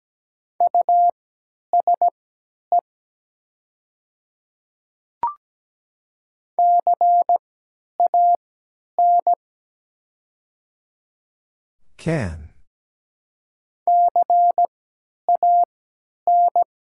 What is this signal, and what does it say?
Morse code, heard as a keyed tone, about 17 wpm, spaced as for 10 wpm, 700 Hz. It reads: USE CAN CAN
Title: 3x Character Spacing - Top 100 Words - 17wpm